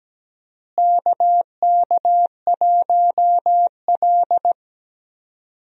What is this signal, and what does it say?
Morse code, heard as a keyed tone, about 17 wpm, 700 Hz. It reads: KK1L
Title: USA Call Signs - 4 Characters - Repeated 2x - 17wpm